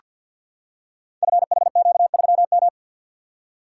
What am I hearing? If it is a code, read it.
Morse code, heard as a keyed tone, about 48 wpm, 700 Hz. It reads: FHQ3M
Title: Code Groups - 5 Letter:Numbers - 50wpm